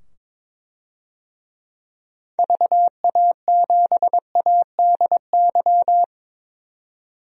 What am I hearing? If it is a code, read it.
Morse code, heard as a keyed tone, about 22 wpm, 700 Hz. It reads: VA7ADY
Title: DXCC-100 Call Signs - 6 Characters - 22wpm